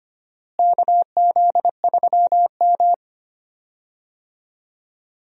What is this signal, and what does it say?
Morse code, heard as a keyed tone, about 25 wpm, 700 Hz. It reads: KZ3M
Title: USA Call Signs - 4 Characters - Repeated 2x - 25wpm